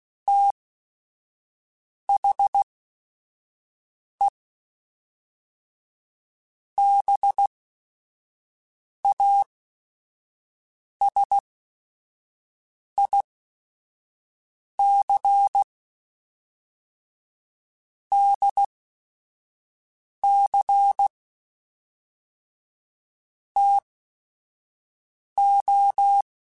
Morse code, heard as a keyed tone, about 16 wpm, spaced as for 5 wpm, 800 Hz. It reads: THE BASIC DC TO
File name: Technical Subjects / Technical Subjects-mp3s / Sw Pwr Sup @5